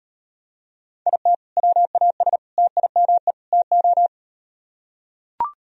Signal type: Morse code

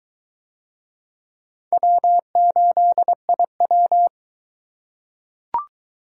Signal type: Morse code

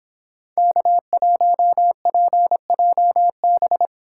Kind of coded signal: Morse code

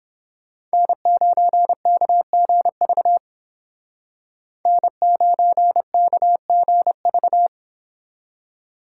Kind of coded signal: Morse code

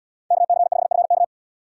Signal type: Morse code

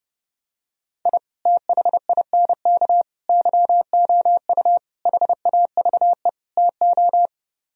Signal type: Morse code

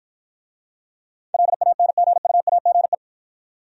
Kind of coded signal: Morse code

